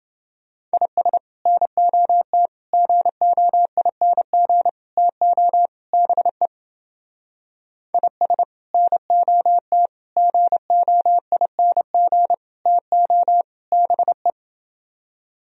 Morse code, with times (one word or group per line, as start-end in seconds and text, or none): 0.7-1.2 IS
1.5-2.5 NOT
2.7-4.7 GOING
5.0-5.7 TO
5.9-6.5 BE
7.9-8.4 IS
8.7-9.9 NOT
10.2-12.3 GOING
12.7-13.4 TO
13.7-14.3 BE